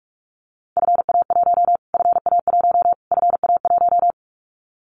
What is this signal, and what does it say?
Morse code, heard as a keyed tone, about 44 wpm, 700 Hz. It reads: FA1 FA1 FA1
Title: Callsign Prefixes for Top 25 DXCC - Repeated 3x - 45wpm